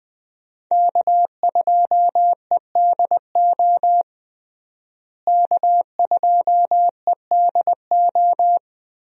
Morse code, heard as a keyed tone, about 20 wpm, 700 Hz. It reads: K2EDO K2EDO
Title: USA Call Signs - 5 Characters - Repeated 3x - 20wpm